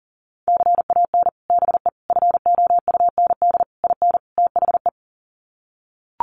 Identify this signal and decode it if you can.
Morse code, heard as a keyed tone, about 40 wpm, 700 Hz. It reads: CAN BE FOUND IN THE
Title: Sets of 5 Words Encore 40wpm